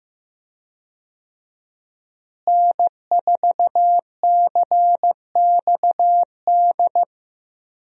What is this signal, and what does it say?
Morse code, heard as a keyed tone, about 15 wpm, 700 Hz. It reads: N4CXD
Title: USA Call Signs - 5 Characters - Repeated 2x - 15wpm